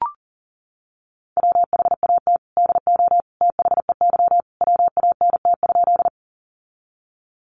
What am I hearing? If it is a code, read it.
Morse code, heard as a keyed tone, about 40 wpm, 700 Hz. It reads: WHAT DO THEY WANT?
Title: Sentences from Top 200 Words 40wpm